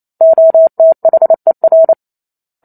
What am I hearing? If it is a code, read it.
Morse code, heard as a keyed tone, about 28 wpm, 650 Hz. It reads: OTHER